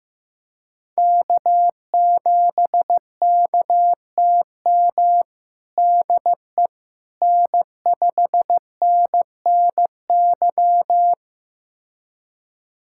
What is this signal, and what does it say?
Morse code, heard as a keyed tone, about 15 wpm, 700 Hz. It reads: K7KTM DE N5NNY
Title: USA Callsign DE Callsign 15wpm